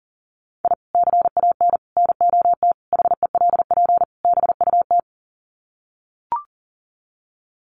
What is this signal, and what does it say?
Morse code, heard as a keyed tone, about 40 wpm, 700 Hz. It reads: I CAN NOT HELP BUT
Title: Sets of 5 Words Encore2 40wpm